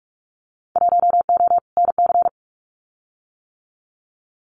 Morse code, heard as a keyed tone, about 46 wpm, 700 Hz. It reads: 1O NC
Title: Winter Field Day Exchanges 45wpm